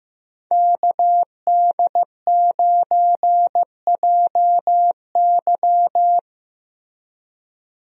Morse code, heard as a keyed tone, about 16 wpm, 700 Hz. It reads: KD9JY
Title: USA Call Signs - 5 Characters - Repeated 2x - 15wpm